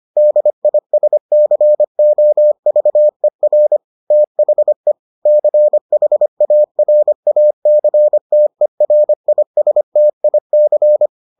Morse code, heard as a keyed tone, about 25 wpm, 600 Hz. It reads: DISCOVER THE CHARACTERISTIC